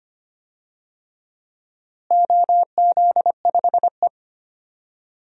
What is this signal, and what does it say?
Morse code, heard as a keyed tone, about 25 wpm, 700 Hz. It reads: OZ5E
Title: DXCC-100 Call Signs - 4 Characters - Repeated 3x - 25wpm